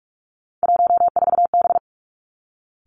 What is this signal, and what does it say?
Morse code, heard as a keyed tone, about 46 wpm, 700 Hz. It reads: 14B
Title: Three Challenging Letter-Number Combinations 45wpm